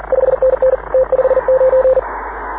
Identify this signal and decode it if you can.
Morse code, heard as a keyed tone, about 40 wpm, 520 Hz, with radio noise. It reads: 5NN T59
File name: AUDIO2019 / HG6N CW SSB MIX